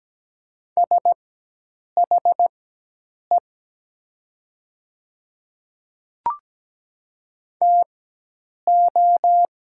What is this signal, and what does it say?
Morse code, heard as a keyed tone, about 17 wpm, spaced as for 8 wpm, 700 Hz. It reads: SHE TO
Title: 4x Character Spacing - Top 100 Words - 17wpm